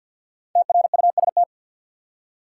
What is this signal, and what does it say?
Morse code, heard as a keyed tone, about 53 wpm, 700 Hz. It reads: TRUST
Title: Five Letter Words - 50wpm